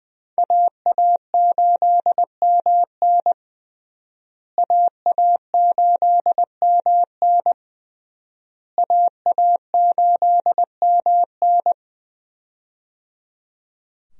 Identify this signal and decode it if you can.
Morse code, heard as a keyed tone, about 20 wpm, 700 Hz. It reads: AA8MN AA8MN AA8MN
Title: USA Call Signs - 5 Characters - Repeated 3x - 20wpm